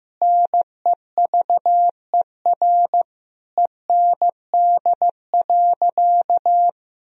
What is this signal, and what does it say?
Morse code, heard as a keyed tone, about 15 wpm, 700 Hz. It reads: NEVER END.
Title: Sentences from Top 1000 Words - Review Part 3 - 15wpm